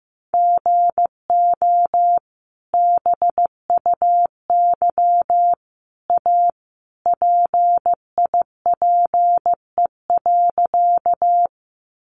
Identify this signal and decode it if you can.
Morse code, heard as a keyed tone, about 15 wpm, 700 Hz. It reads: GO BUY A PIPE.
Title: Sentences from Top 1500 Words 15wpm